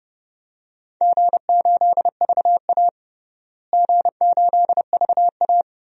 Morse code, heard as a keyed tone, about 30 wpm, 700 Hz. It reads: G8VA G8VA
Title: DXCC-100 Call Signs - 4 Characters - Repeated 3x - 30wpm